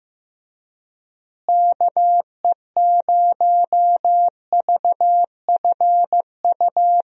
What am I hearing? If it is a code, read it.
Morse code, heard as a keyed tone, about 15 wpm, 700 Hz. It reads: KE0VFU
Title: USA Call Signs - 6 Characters - Repeated 2x - 15wpm